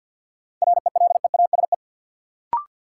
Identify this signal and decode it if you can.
Morse code, heard as a keyed tone, about 53 wpm, 700 Hz. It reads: RELEASE
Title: Sentences from Top 1100 Words - New Word Drill Alt - 50wpm